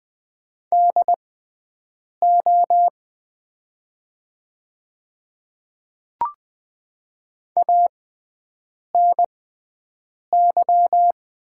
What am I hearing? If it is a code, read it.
Morse code, heard as a keyed tone, about 20 wpm, spaced as for 7 wpm, 690 Hz. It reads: DO ANY
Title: 6x Character Spacing - Top 300 Words - 20wpm